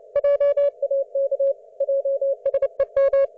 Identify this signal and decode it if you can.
Morse code, heard as a keyed tone, about 29 wpm, 560 Hz. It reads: JAK JSEM